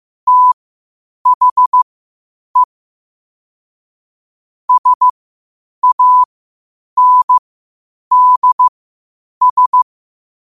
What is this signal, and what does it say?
Morse code, heard as a keyed tone, about 15 wpm, spaced as for 8 wpm, 1000 Hz. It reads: THE SANDS
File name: qotdfiles / 8wpm